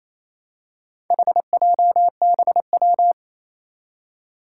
Morse code, heard as a keyed tone, about 27 wpm, 700 Hz. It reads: HJBW
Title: Four Challenging Letter-Number Combinations 28wpm